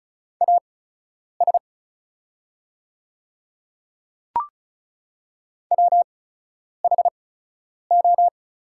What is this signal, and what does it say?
Morse code, heard as a keyed tone, about 36 wpm, spaced as for 10 wpm, 700 Hz. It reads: AS WHO
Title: 8x Character Spacing - Top 100 Words - 35wpm